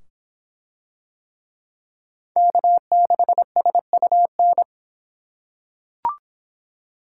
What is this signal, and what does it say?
Morse code, heard as a keyed tone, about 25 wpm, 700 Hz. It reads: K6SUN